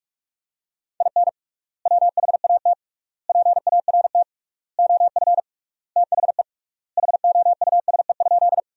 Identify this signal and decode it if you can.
Morse code, heard as a keyed tone, about 41 wpm, 700 Hz. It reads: IN WHAT PART OF THE HOUSE?
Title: 3x Word-Spacing - Sentences from Top 200 Words - 45wpm